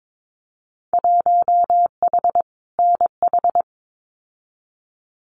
Morse code, heard as a keyed tone, about 22 wpm, 700 Hz. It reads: 1H NH
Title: Winter Field Day Exchanges 22wpm